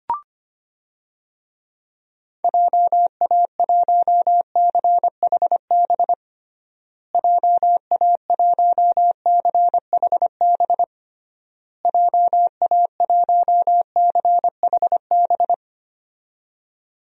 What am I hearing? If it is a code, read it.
Morse code, heard as a keyed tone, about 25 wpm, 700 Hz. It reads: JA1CHB JA1CHB JA1CHB
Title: DXCC-100 Call Signs - 6 Characters - Repeated 3x - 25wpm